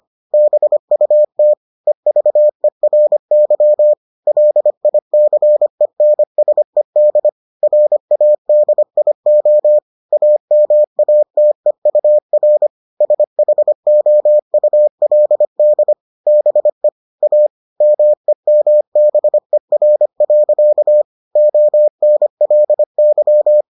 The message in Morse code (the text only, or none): BUT EVERY LICENSED RADIO AMATEUR SHOULD BE A MEMBER. ONLY